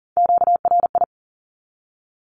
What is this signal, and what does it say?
Morse code, heard as a keyed tone, about 40 wpm, 700 Hz. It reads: QRI
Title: Q-Codes 40wpm